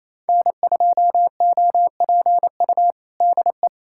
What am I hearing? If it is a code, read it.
Morse code, heard as a keyed tone, about 28 wpm, 700 Hz. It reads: N2OPU DE
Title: USA Callsign DE Callsign 28wpm